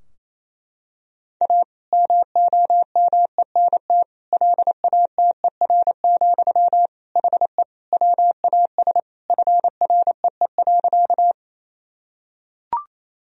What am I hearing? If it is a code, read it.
Morse code, heard as a keyed tone, about 28 wpm, 700 Hz. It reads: A MOMENT LATER, HE WAS FREE.
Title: Sentences from Top 1000 Words - Review Part 4 - 28wpm